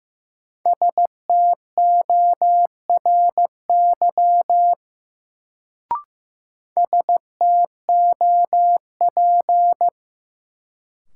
Morse code, heard as a keyed tone, about 15 wpm, 700 Hz. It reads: STORY STOP
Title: Sentences from Top 300 Words - New Word Drill - 15wpm